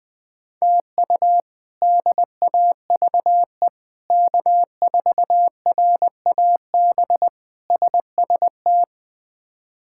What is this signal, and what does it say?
Morse code, heard as a keyed tone, about 20 wpm, 700 Hz. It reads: TU DAVE K4RAB SST
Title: K1USN SST QSO Practice - 20wpm